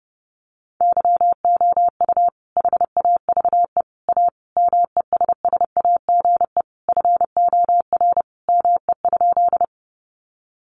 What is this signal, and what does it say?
Morse code, heard as a keyed tone, about 30 wpm, 700 Hz. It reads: YOU HAVE A MESSAGE FOR ME?
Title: Sentences from Top 1100 Words 30wpm